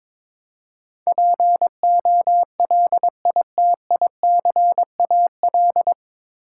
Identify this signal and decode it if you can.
Morse code, heard as a keyed tone, about 22 wpm, 700 Hz. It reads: POLITICAL
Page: Morse Code Ninja practice 22wpm